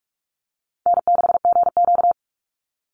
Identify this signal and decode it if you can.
Morse code, heard as a keyed tone, about 44 wpm, 700 Hz. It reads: N6GQ
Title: Easy Call Signs - 45wpm